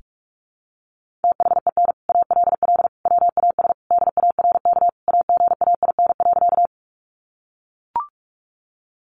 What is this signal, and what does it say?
Morse code, heard as a keyed tone, about 44 wpm, 700 Hz. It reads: THEN ALL WAS DARK AGAIN.
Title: Sentences from Top 500 Words 45wpm